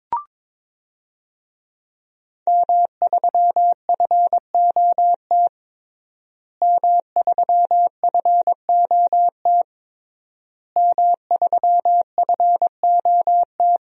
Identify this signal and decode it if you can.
Morse code, heard as a keyed tone, about 22 wpm, 700 Hz. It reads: M3FOT M3FOT M3FOT
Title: DXCC-100 Call Signs - 5 Characters - Repeated 3x - 22wpm